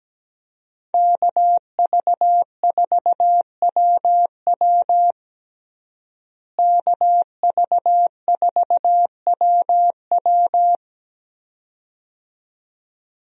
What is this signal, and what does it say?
Morse code, heard as a keyed tone, about 17 wpm, 700 Hz. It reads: KV4WW KV4WW
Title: USA Call Signs - 5 Characters - Repeated 2x - 17wpm